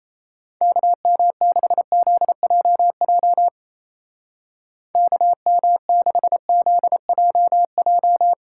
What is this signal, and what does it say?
Morse code, heard as a keyed tone, about 36 wpm, 700 Hz. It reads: KM6ZJJ KM6ZJJ